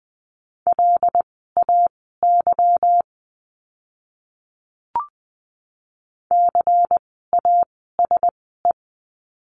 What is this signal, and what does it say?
Morse code, heard as a keyed tone, about 20 wpm, spaced as for 15 wpm, 700 Hz. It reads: LAY CASE
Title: 146 Lesson 24f - Letter y - Words 20wpm